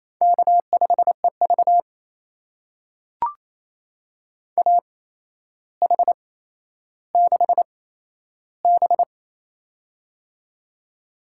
Morse code, heard as a keyed tone, about 28 wpm, spaced as for 12 wpm, 700 Hz. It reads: K5EV AH6B
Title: USA Call Signs - 4 Characters - 8x Character Spacing - 28wpm